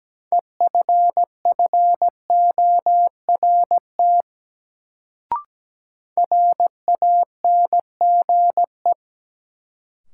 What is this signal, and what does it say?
Morse code, heard as a keyed tone, about 17 wpm, 700 Hz. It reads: EFFORT RANGE
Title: Sets of 4 Words - New Word Drill - 17wpm